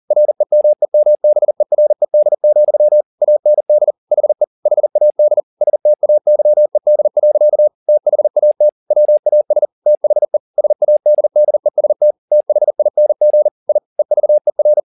Morse code, heard as a keyed tone, about 40 wpm, 600 Hz. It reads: REMEMBERED, AND HE HAD STAYED. THAT WAS THE SADDEST THING I EVER